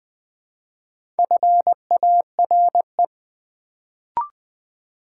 Morse code, heard as a keyed tone, about 20 wpm, 700 Hz. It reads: FARE
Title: Build A Word - 1 Letter At A Time - 3 to 5-Letter Words - 20wpm